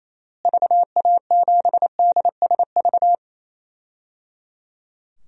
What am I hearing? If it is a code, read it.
Morse code, heard as a keyed tone, about 28 wpm, 700 Hz. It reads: VA7DSV
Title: DXCC-100 Call Signs - 6 Characters - Repeated 3x - 28wpm